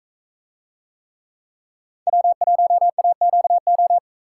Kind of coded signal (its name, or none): Morse code